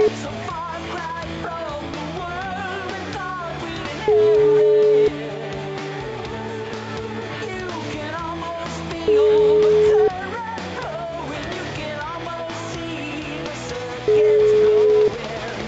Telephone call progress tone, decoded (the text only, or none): ringback tone